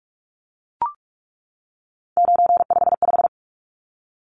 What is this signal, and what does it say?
Morse code, heard as a keyed tone, about 44 wpm, 700 Hz. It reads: 955
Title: Numbers - 3-Digit 45wpm